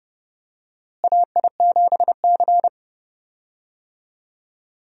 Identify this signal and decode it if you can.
Morse code, heard as a keyed tone, about 30 wpm, 700 Hz. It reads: AI7C
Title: USA Call Signs - 4 Characters - Repeated 2x - 30wpm